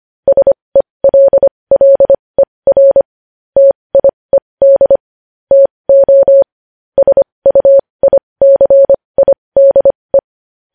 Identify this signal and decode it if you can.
Morse code, heard as a keyed tone, about 25 wpm, spaced as for 20 wpm, 550 Hz. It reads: SELLER TIED TO SUICIDE
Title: News Headlines at 20